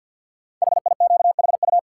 Morse code, heard as a keyed tone, about 47 wpm, 700 Hz. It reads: HIQHU
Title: Code Groups - 5 Letters - 50wpm